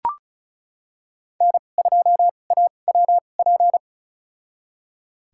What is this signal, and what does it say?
Morse code, heard as a keyed tone, about 36 wpm, spaced as for 25 wpm, 700 Hz. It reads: N2AWP
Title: USA Call Signs - 5 Characters - 2x Character Spacing - 35wpm